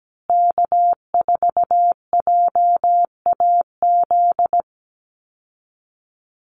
Morse code, heard as a keyed tone, about 17 wpm, 700 Hz. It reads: K4JAZ
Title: Call Signs Repeated 3x - 17wpm